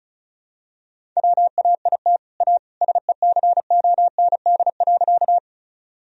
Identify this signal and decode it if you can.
Morse code, heard as a keyed tone, about 36 wpm, 700 Hz. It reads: WAIT A SECOND.
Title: Sentences from Top 1000 Words - Review Part 1 - 35wpm